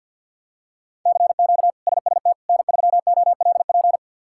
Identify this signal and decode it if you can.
Morse code, heard as a keyed tone, about 50 wpm, 700 Hz. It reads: CQ SST N3OLP
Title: K1USN SST QSO Practice - 50wpm